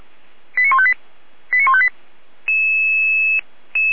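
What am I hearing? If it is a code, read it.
Five-tone selective call (ZVEI), heard as quick runs of five tones, tones of about 70 ms. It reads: 89158 89158
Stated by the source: ZVEI five-tone call 89163 received by SDR.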